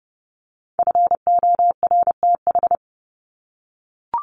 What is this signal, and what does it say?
Morse code, heard as a keyed tone, about 30 wpm, 700 Hz. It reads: FORTH
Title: Top 1751-2000 Words 30wpm